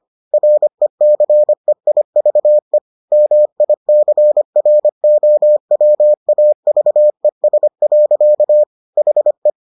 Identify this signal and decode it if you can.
Morse code, heard as a keyed tone, about 25 wpm, 600 Hz. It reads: RECEIVE MICROWAVES. HE